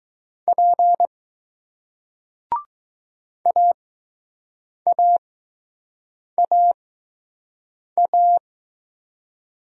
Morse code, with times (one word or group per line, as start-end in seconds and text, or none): 0.5-1.1 P
3.5-8.4 AAAA